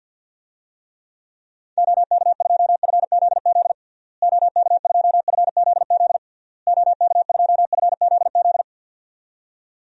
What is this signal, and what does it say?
Morse code, heard as a keyed tone, about 51 wpm, 700 Hz. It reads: OK2FZZ OK2FZZ OK2FZZ
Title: DXCC-100 Call Signs - 6 Characters - Repeated 3x - 50wpm